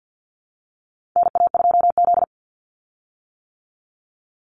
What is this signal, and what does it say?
Morse code, heard as a keyed tone, about 50 wpm, 700 Hz. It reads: NA2Z
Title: USA Call Signs 50wpm